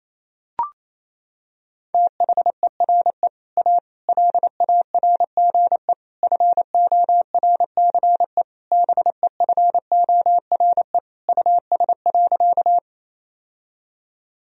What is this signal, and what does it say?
Morse code, heard as a keyed tone, about 28 wpm, 700 Hz. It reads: THERE A LARGE FORCE BEFORE US.
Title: Sentences from Top 1000 Words - Review Part 3 - 28wpm